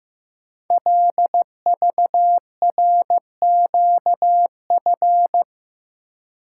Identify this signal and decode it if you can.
Morse code, heard as a keyed tone, about 15 wpm, 700 Hz. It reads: LVRQF
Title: Code Groups - 5 Letters - 15wpm